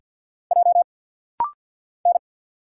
Morse code, heard as a keyed tone, about 48 wpm, 700 Hz. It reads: J N
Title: Single Letter-Number - Rapid-Fire 50wpm